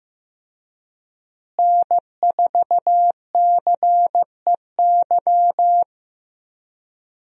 Morse code, heard as a keyed tone, about 16 wpm, 700 Hz. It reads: N4CEY
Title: USA Call Signs - 5 Characters - Repeated 2x - 15wpm